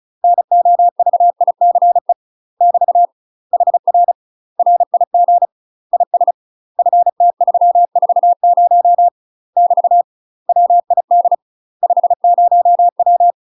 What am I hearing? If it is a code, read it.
Morse code, heard as a keyed tone, about 36 wpm, 700 Hz. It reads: NOVICE <BT> HR RIG IS FT340 <BT> WID 50W
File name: UK MASTER QSOs TRAINING / 35WPM / UK MASTER QSOs 0010